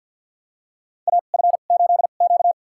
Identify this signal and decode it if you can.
Morse code, heard as a keyed tone, about 50 wpm, spaced as for 36 wpm, 700 Hz. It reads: AF8Q
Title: USA Call Signs - 4 Characters - 2x Character Spacing - 50wpm